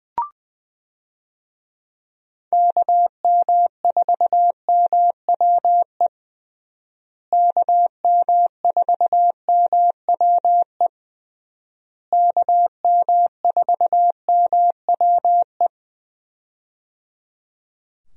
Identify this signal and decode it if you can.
Morse code, heard as a keyed tone, about 20 wpm, 700 Hz. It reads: KM4MWE KM4MWE KM4MWE
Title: USA Call Signs - 6 Characters - Repeated 3x - 20wpm